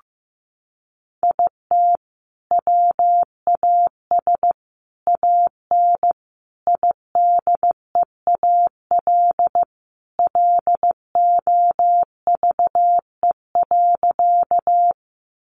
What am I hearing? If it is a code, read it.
Morse code, heard as a keyed tone, about 15 wpm, 700 Hz. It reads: IT WAS AN IDEAL LOVE.
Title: Sentences from Top 1400 Words 15wpm